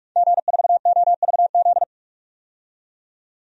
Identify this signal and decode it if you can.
Morse code, heard as a keyed tone, about 46 wpm, 700 Hz. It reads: G4OVZ
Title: DXCC-100 Call Signs - 5 Characters - 45wpm